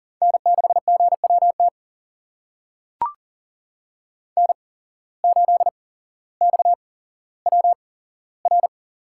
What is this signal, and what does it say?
Morse code, heard as a keyed tone, about 39 wpm, spaced as for 17 wpm, 700 Hz. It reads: N6GWT N8XWR
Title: USA Call Signs - 5 Characters - 8x Character Spacing - 40wpm